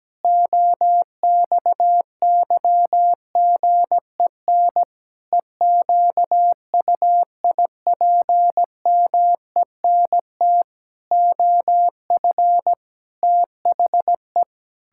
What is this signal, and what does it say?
Morse code, heard as a keyed tone, about 17 wpm, 700 Hz. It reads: OXYGEN EQUIPMENT OF THE
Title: Pangram Challenge v2 - 17wpm